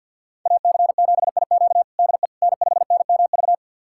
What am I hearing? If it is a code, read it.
Morse code, heard as a keyed tone, about 51 wpm, 700 Hz. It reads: AC7IQ DE N5NMV
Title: USA Callsign DE Callsign - 5 Characters - 50wpm